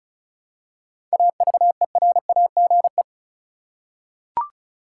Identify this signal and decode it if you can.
Morse code, heard as a keyed tone, about 36 wpm, 700 Hz. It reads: AVERAGE
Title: Seven Letter Words - 35wpm